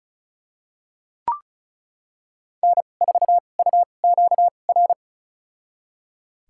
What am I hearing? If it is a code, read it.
Morse code, heard as a keyed tone, about 36 wpm, spaced as for 25 wpm, 700 Hz. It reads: N4UQR
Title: USA Call Signs - 5 Characters - 2x Character Spacing - 35wpm